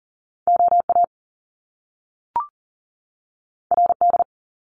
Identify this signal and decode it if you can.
Morse code, heard as a keyed tone, about 40 wpm, 700 Hz. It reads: OA RD
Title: Top Two Letter Combinations in Words 40wpm